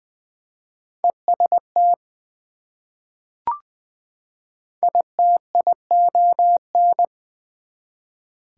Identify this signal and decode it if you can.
Morse code, heard as a keyed tone, about 20 wpm, 700 Hz. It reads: EST ITION
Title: Prefixes and Suffixes 20wpm